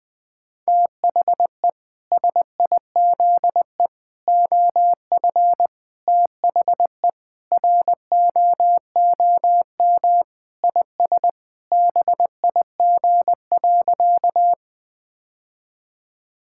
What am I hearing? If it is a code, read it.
Morse code, heard as a keyed tone, about 20 wpm, 700 Hz. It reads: THE SIZE OF THE ROOM IS BIG.
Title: Sentences with 3-Word Phrases - 20wpm